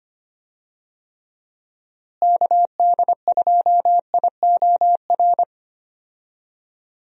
Morse code, heard as a keyed tone, about 25 wpm, 700 Hz. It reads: KD2IOR
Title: USA Call Signs - 6 Characters - Repeated 2x - 25wpm